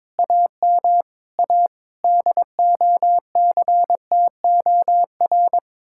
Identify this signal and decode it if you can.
Morse code, heard as a keyed tone, about 24 wpm, 700 Hz. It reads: AM A DOCTOR
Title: Speed-Racing Common QSO Phrases - 22wpm